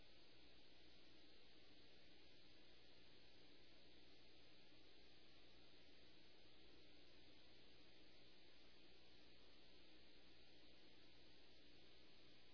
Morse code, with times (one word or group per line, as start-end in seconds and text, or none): none